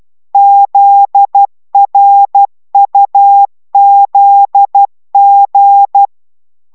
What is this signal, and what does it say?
Morse code, heard as a keyed tone, about 12 wpm, 800 Hz. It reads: ZRUZG
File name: taviro tanulas / cw 69 15 10